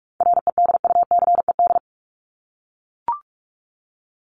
Morse code, heard as a keyed tone, about 44 wpm, 700 Hz. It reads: REDUCED
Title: ED Suffix - Top 100 Words with Base Words 45wpm